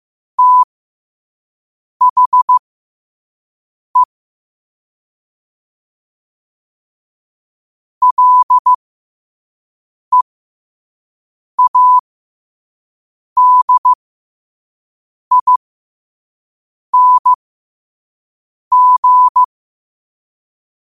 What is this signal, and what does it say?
Morse code, heard as a keyed tone, about 15 wpm, spaced as for 5 wpm, 1000 Hz. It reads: THE LEADING
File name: qotdfiles / 5wpm